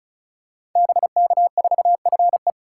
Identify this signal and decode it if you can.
Morse code, heard as a keyed tone, about 36 wpm, 700 Hz. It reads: BK4FE
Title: Code Groups - 5 Letter:Numbers - 35wpm